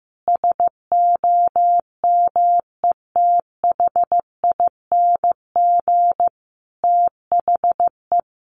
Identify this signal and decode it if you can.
Morse code, heard as a keyed tone, about 15 wpm, 700 Hz. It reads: SOMETHING THE
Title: Sentences from Top 700 Words 15wpm